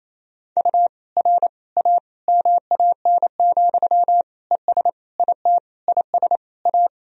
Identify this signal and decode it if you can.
Morse code, heard as a keyed tone, about 28 wpm, 700 Hz. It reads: U R A MAN, ES IT IS A
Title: Sentences from Top 200 Words and Abbreviations 28wpm